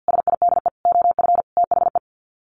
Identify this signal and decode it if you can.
Morse code, heard as a keyed tone, about 50 wpm, 700 Hz. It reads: SIDE OF THE
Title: Sets of 5 Words Encore 50wpm